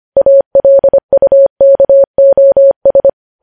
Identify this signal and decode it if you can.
Morse code, heard as a keyed tone, about 25 wpm, 550 Hz. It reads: ALUKOS
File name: podcast / News headlines-1x-25x25